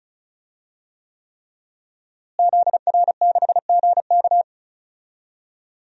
Morse code, heard as a keyed tone, about 34 wpm, 700 Hz. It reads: ZR6GK
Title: DXCC-100 Call Signs - 5 Characters - Repeated 3x - 35wpm